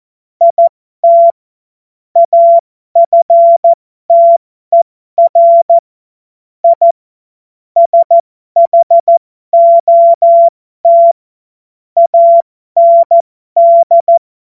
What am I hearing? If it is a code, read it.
Morse code, heard as a keyed tone, about 14 wpm, spaced as for 12 wpm, 680 Hz. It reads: IT AFTER I SHOT AND